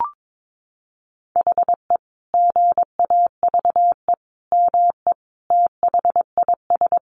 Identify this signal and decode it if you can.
Morse code, heard as a keyed tone, about 22 wpm, 700 Hz. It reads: HE GAVE ME THIS
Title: Sentences from Top 500 Words 22wpm